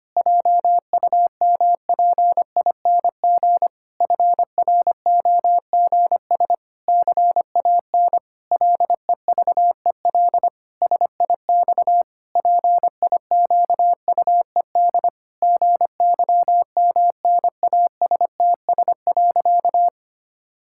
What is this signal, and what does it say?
Morse code, heard as a keyed tone, about 25 wpm, 700 Hz. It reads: JUMPING FROGS CAN LEVEL SIX PIQUED GYMNASTS.
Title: Pangram Challenge v2 - 25wpm